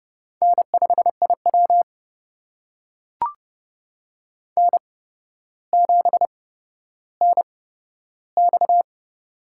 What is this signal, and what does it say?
Morse code, heard as a keyed tone, about 30 wpm, spaced as for 13 wpm, 700 Hz. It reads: N5IW N7NX
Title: USA Call Signs - 4 Characters - 8x Character Spacing - 30wpm